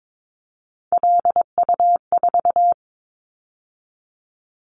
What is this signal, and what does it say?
Morse code, heard as a keyed tone, about 22 wpm, 700 Hz. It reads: LU4
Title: Callsign Prefixes for Top 100 DXCC - 22wpm